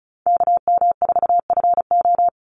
Morse code, heard as a keyed tone, about 34 wpm, 700 Hz. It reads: KM4FO
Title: Call Signs - 35wpm